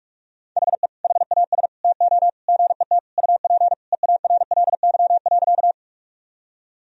Morse code, heard as a keyed tone, about 46 wpm, 700 Hz. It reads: HE HAS TO GET UP EARLY.
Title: Sentences from Top 1000 Words - Review Part 2 - 45wpm